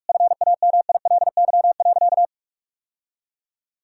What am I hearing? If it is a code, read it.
Morse code, heard as a keyed tone, about 46 wpm, 700 Hz. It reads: FAMILY.
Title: Sentences from Top 1000 Words - Review Part 5 - 45wpm